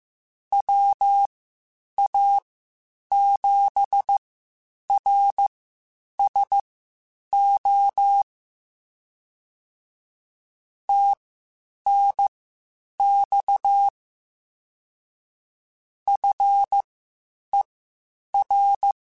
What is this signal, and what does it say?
Morse code, heard as a keyed tone, about 15 wpm, spaced as for 8 wpm, 800 Hz. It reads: WA7RSO TNX FER
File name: (spaced) / QSOs Spaced-mp3s / kc7haa @13 sp